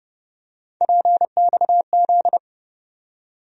Morse code, heard as a keyed tone, about 30 wpm, 700 Hz. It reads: PXZ
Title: USA Call Sign Postfixes - 3 Letters 30wpm